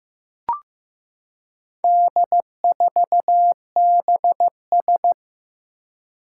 Morse code, heard as a keyed tone, about 15 wpm, 700 Hz. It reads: D4BS
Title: Four Challenging Letter-Number Combinations 15wpm